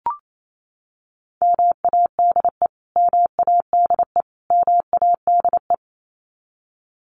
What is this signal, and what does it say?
Morse code, heard as a keyed tone, about 28 wpm, 700 Hz. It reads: MADE MADE MADE
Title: Top 200 Words Repeated 28wpm